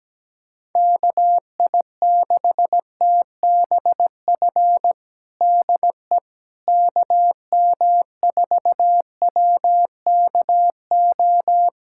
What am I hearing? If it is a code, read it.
Morse code, heard as a keyed tone, about 17 wpm, 700 Hz. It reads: KI6TBF DE KM4WKO
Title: USA Callsign DE Callsign 17wpm